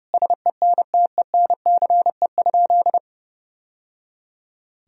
Morse code, heard as a keyed tone, about 30 wpm, 700 Hz. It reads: SENTENCE?
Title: Sentences from Top 1000 Words - Review Part 1 - 30wpm